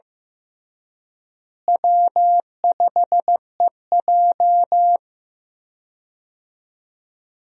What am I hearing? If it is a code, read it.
Morse code, heard as a keyed tone, about 16 wpm, 700 Hz. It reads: W5EJ
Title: USA Call Signs - 4 Characters - Repeated 3x - 15wpm